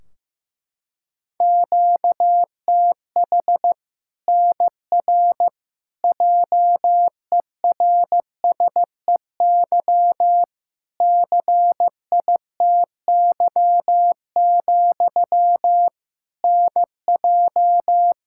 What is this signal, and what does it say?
Morse code, heard as a keyed tone, about 15 wpm, 700 Hz. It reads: QTH NR JERSEY CITY, NJ
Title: K1USN SST QSO Practice 15wpm